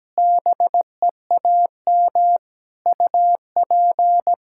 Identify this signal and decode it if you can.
Morse code, heard as a keyed tone, about 17 wpm, 700 Hz. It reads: BEAM UP
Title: Practice QSO - Exchange 2 - Info - 17wpm